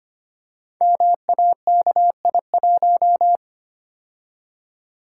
Morse code, heard as a keyed tone, about 25 wpm, 700 Hz. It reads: MAKI1
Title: Code Groups - 5 Letter:Numbers - 25wpm